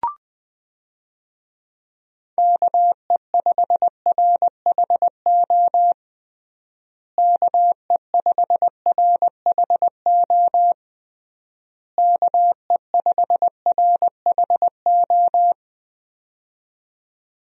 Morse code, heard as a keyed tone, about 20 wpm, 700 Hz. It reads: KE5RHO KE5RHO KE5RHO
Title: USA Call Signs - 6 Characters - Repeated 3x - 20wpm